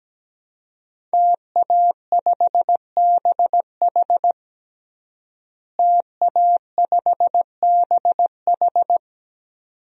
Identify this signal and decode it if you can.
Morse code, heard as a keyed tone, about 17 wpm, 700 Hz. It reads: TA5BH TA5BH
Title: DXCC-100 Call Signs - 5 Characters - Repeated 2x - 17wpm